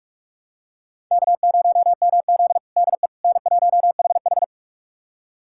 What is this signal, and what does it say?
Morse code, heard as a keyed tone, about 44 wpm, 690 Hz. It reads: K0MZ DE N1HH
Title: USA Callsign DE Callsign - 4 Characters - 45wpm